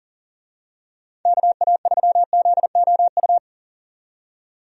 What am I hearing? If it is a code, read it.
Morse code, heard as a keyed tone, about 40 wpm, 700 Hz. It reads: KA3ZOU
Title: USA Call Signs - 6 Characters - 40wpm